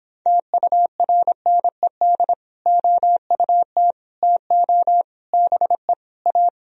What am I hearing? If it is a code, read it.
Morse code, heard as a keyed tone, about 26 wpm, 700 Hz. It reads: TURNED OUT TO BE A